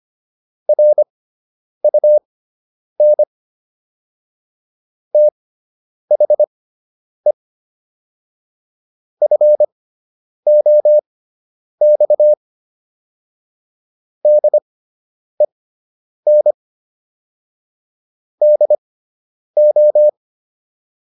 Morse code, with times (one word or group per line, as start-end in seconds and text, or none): 0.7-3.2 RUN
5.1-7.3 THE
9.2-12.3 FOX
14.2-16.5 DEN
18.4-20.1 DO